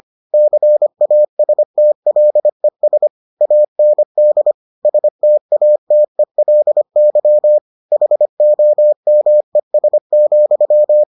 Morse code, heard as a keyed tone, about 25 wpm, 600 Hz. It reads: CASTLES AND STATELY HOMES,